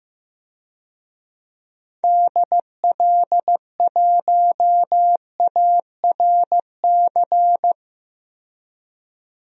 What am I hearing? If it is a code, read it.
Morse code, heard as a keyed tone, about 15 wpm, 700 Hz. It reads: DL1ARC
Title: DXCC-100 Call Signs - 6 Characters - 15wpm